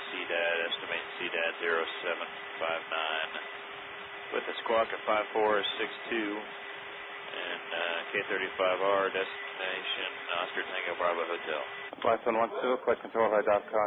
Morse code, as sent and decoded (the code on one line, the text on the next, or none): none